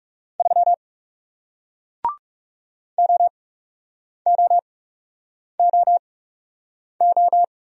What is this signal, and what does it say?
Morse code, heard as a keyed tone, about 40 wpm, spaced as for 8 wpm, 700 Hz. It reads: 3 OOOO